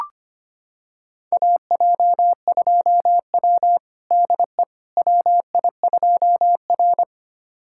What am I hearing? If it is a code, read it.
Morse code, heard as a keyed tone, about 25 wpm, 700 Hz. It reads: AJ2W DE WI2R
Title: USA Callsign DE Callsign - 4 Characters - 25wpm